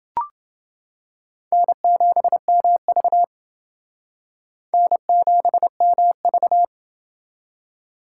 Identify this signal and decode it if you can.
Morse code, heard as a keyed tone, about 28 wpm, 700 Hz. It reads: N7MV N7MV